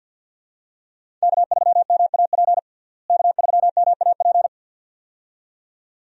Morse code, heard as a keyed tone, about 48 wpm, 700 Hz. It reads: K3MAP K3MAP
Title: USA Call Signs - 5 Characters - Repeated 2x - 50wpm